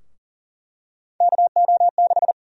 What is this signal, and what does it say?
Morse code, heard as a keyed tone, about 40 wpm, 700 Hz. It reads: KO6
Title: USA Call Sign Prefixes - 40wpm